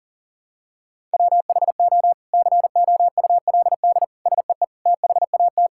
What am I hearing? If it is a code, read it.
Morse code, heard as a keyed tone, about 40 wpm, 700 Hz. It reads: WHO COULD SEE THAT
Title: Sentences from Top 1600 Words 40wpm